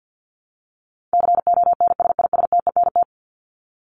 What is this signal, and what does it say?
Morse code, heard as a keyed tone, about 50 wpm, 700 Hz. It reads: CONSISTENT
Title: Top 1251-1500 Words 50wpm